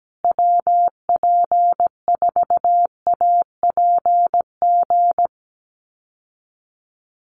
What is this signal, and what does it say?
Morse code, heard as a keyed tone, about 17 wpm, 700 Hz. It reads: WP4APG
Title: USA Call Signs Repeated 3x 17wpm